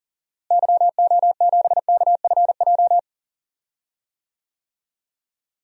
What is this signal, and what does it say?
Morse code, heard as a keyed tone, about 40 wpm, 700 Hz. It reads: YO7KFJ
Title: DXCC-100 Call Signs - 6 Characters - Repeated 2x - 40wpm